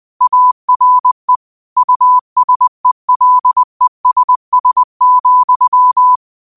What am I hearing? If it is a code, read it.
Morse code, heard as a keyed tone, about 20 wpm, 1000 Hz. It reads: ARE USELESS,
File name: qotdfiles / 20wpm